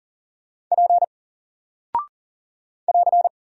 Morse code, heard as a keyed tone, about 40 wpm, 700 Hz. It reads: P <AR>
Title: Single Character (Letter:Number:Punctuation:Prosign) Alt 40wpm